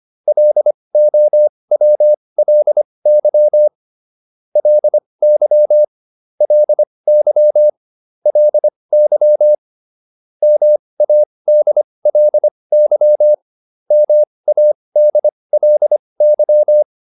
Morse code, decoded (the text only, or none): LOWLY LY LY LY MADLY MADLY